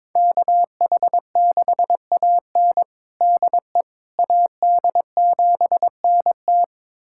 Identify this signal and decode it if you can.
Morse code, heard as a keyed tone, about 22 wpm, 700 Hz. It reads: KH6AN DE AD7NT
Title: USA Callsign DE Callsign 22wpm